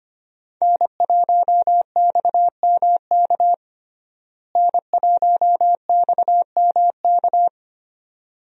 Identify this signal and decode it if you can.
Morse code, heard as a keyed tone, about 25 wpm, 700 Hz. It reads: N1XMK N1XMK
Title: USA Call Signs - 5 Characters - Repeated 2x - 25wpm